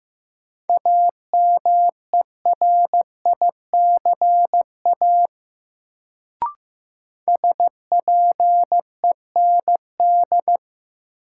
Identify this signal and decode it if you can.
Morse code, heard as a keyed tone, about 15 wpm, 700 Hz. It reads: AMERICA SPEND
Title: Sets of 5 Words - New Word Drill - 15wpm